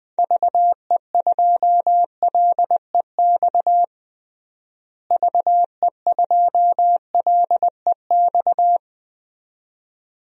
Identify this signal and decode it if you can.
Morse code, heard as a keyed tone, about 20 wpm, 700 Hz. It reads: VE2LEX VE2LEX
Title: DXCC-100 Call Signs - 6 Characters - Repeated 2x - 20wpm